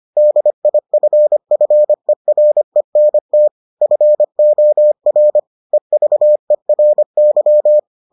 Morse code, heard as a keyed tone, about 25 wpm, 600 Hz. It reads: DIFFERENT FOR EVERY